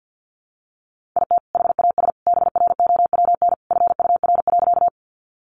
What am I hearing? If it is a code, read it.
Morse code, heard as a keyed tone, about 50 wpm, 700 Hz. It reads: IT HAS BROWN FUR.
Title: Easy Phrases 50wpm